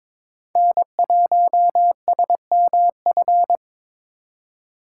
Morse code, heard as a keyed tone, about 23 wpm, 700 Hz. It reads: N1SMF
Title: USA Call Signs - 4x Character Spacing - 22wpm